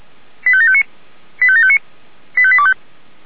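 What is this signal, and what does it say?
Five-tone selective call (ZVEI), heard as quick runs of five tones, tones of about 70 ms. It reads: 85759 85759 85825
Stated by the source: ZVEI five-tone selective call 85729 received with an SDR.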